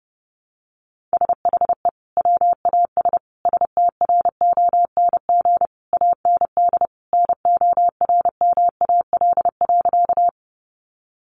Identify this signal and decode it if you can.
Morse code, heard as a keyed tone, about 30 wpm, 700 Hz. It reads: SHE WAS STRONG AND NORMAL.
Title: Sentences from Top 1200 Words 30wpm